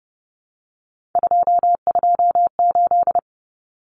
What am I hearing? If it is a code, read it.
Morse code, heard as a keyed tone, about 30 wpm, 700 Hz. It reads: 228
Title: Numbers - 3-Digit 30wpm